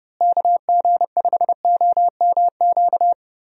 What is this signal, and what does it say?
Morse code, heard as a keyed tone, about 30 wpm, 700 Hz. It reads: KG5OMQ
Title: USA Call Signs - 6 Characters - Repeated 2x - 30wpm